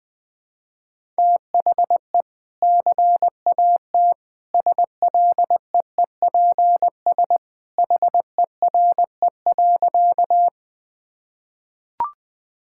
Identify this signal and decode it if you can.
Morse code, heard as a keyed tone, about 21 wpm, 700 Hz. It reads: THE CAT SLEEPS HERE.
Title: Easy Sentences with Hints - 20wpm